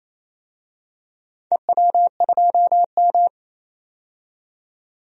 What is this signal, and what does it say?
Morse code, heard as a keyed tone, about 28 wpm, 700 Hz. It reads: EW2M
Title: DXCC-100 Call Signs - 4 Characters - 28wpm